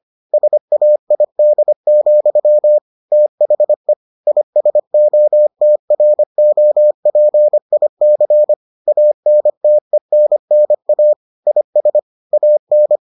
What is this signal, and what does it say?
Morse code, heard as a keyed tone, about 25 wpm, 600 Hz. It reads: SAID, THE ISOTROPIC ANTENNA IS AN